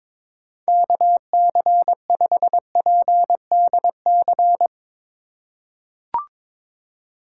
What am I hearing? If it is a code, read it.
Morse code, heard as a keyed tone, about 23 wpm, 700 Hz. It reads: KC5PDC
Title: USA Call Signs - 6 Characters - 2x Character Spacing - 22wpm